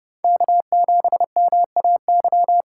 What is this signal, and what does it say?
Morse code, heard as a keyed tone, about 34 wpm, 700 Hz. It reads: K7MAY